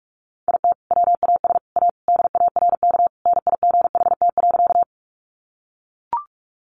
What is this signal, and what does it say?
Morse code, heard as a keyed tone, about 44 wpm, 700 Hz. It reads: IT WAS A DARK NIGHT.
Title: Sentences from Top 500 Words 45wpm